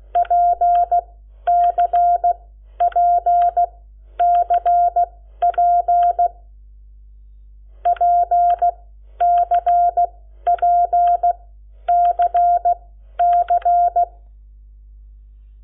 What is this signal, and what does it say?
Morse code, heard as a keyed tone, about 16 wpm, spaced as for 11 wpm, 700 Hz. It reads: PCPCP PCPCC
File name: Lecciones / CLASE Nº7